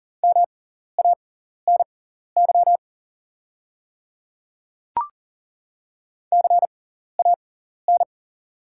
Morse code, heard as a keyed tone, about 40 wpm, spaced as for 14 wpm, 690 Hz. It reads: MANY CAN
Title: 6x Character Spacing - Top 500 Words - 40wpm